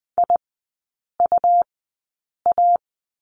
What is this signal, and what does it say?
Morse code, heard as a keyed tone, about 20 wpm, spaced as for 8 wpm, 700 Hz. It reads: IUA
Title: 100 Lesson 17e - Letter u - Three Characters 20wpm